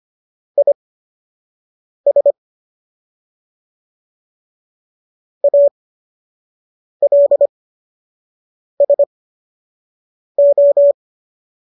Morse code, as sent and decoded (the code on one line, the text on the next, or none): .. ... / .- .-.. ... ---
IS ALSO